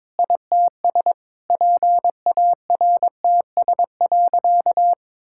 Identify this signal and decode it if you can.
Morse code, heard as a keyed tone, about 22 wpm, 700 Hz. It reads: ITS PARTS.
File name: practice / 22wpm / Sentences with 8-Word Phrases - 22wpm